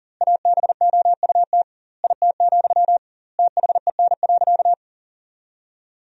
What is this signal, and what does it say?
Morse code, heard as a keyed tone, about 40 wpm, 700 Hz. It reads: ABOUT IT, THEN.
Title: 2x Word-Spacing - Sentences from Top 200 Words - 40wpm